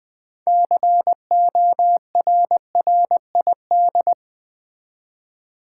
Morse code, heard as a keyed tone, about 21 wpm, 700 Hz. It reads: CORRID
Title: Build A Word - 2 Letters At A Time - 8-Letter Words - 20wpm